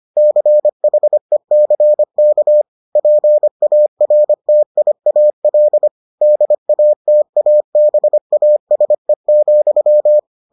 Morse code, heard as a keyed tone, about 25 wpm, 600 Hz. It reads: CHECK PARTIAL DATABASE,